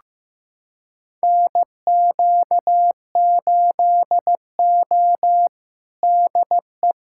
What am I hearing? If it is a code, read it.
Morse code, heard as a keyed tone, about 16 wpm, 700 Hz. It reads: NQ8O DE
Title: USA Callsign DE Callsign - 4 Characters - 15wpm